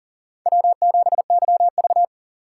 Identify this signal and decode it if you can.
Morse code, heard as a keyed tone, about 40 wpm, 700 Hz. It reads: W7YV
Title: USA Call Signs - 4 Characters - 4x Character Spacing - 40wpm